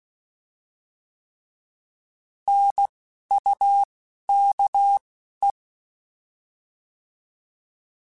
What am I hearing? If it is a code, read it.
Morse code, heard as a keyed tone, about 16 wpm, spaced as for 11 wpm, 800 Hz. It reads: NUKE